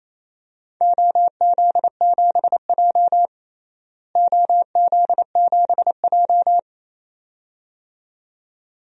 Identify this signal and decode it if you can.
Morse code, heard as a keyed tone, about 28 wpm, 700 Hz. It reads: OZ7J OZ7J
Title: DXCC-100 Call Signs - 4 Characters - Repeated 2x - 28wpm